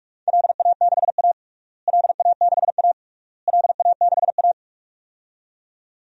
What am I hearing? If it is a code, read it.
Morse code, heard as a keyed tone, about 46 wpm, 700 Hz. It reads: LA6A LA6A LA6A
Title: DXCC-100 Call Signs - 4 Characters - Repeated 3x - 45wpm